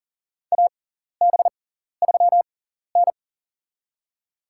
Morse code, heard as a keyed tone, about 40 wpm, spaced as for 14 wpm, 700 Hz. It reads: AB3N
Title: USA Call Signs - 4 Characters - 6x Character Spacing - 40wpm